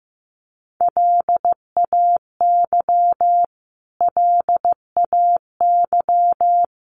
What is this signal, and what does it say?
Morse code, heard as a keyed tone, about 15 wpm, 700 Hz. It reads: LAY LAY
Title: Top 500 Words Repeated 15wpm